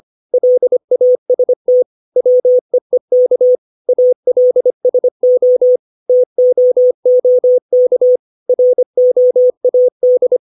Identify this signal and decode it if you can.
Morse code, heard as a keyed tone, about 25 wpm, 500 Hz. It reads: LAST WEEK ALSO TOOK ROAD
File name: 632 words / 632 Words 25WPM